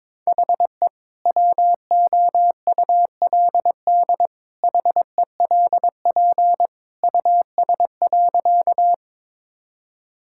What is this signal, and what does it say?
Morse code, heard as a keyed tone, about 21 wpm, 700 Hz. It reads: HE WOULD HELP US.
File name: practice / 22wpm / Sentences with 2-Word Phrases - 22wpm